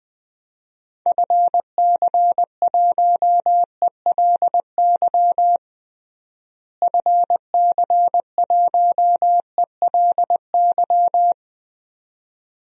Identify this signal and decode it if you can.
Morse code, heard as a keyed tone, about 19 wpm, 700 Hz. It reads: FC1ELY FC1ELY
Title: DXCC-100 Call Signs - 6 Characters - Repeated 2x - 20wpm